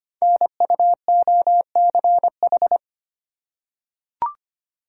Morse code, heard as a keyed tone, about 25 wpm, 700 Hz. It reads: NUOCH
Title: Code Groups - 5 Letters - 25wpm